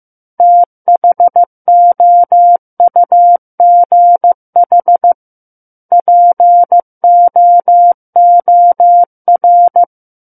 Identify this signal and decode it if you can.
Morse code, heard as a keyed tone, about 15 wpm, 700 Hz. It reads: THOUGH POOR
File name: Book One / 06-Chapter 6 - 15 WPM